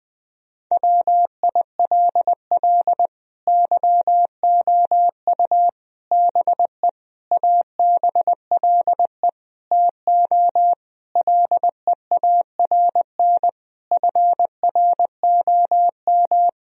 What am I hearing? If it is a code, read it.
Morse code, heard as a keyed tone, about 21 wpm, 700 Hz. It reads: WILL YOU BE ABLE TO LEARN FROM
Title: Sentences from Top 1000 Words - Review Part 2 - 20wpm